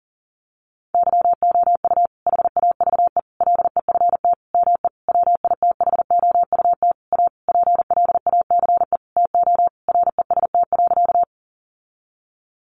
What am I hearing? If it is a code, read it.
Morse code, heard as a keyed tone, about 37 wpm, 700 Hz. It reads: YOU HAVE LEFT ME WITHOUT A PLACE TO REST.
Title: Sentences from Top 600 Words 40wpm